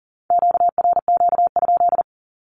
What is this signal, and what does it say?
Morse code, heard as a keyed tone, about 41 wpm, 700 Hz. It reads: QRQ?
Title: Q-Codes with Spoken Meaning - 40wpm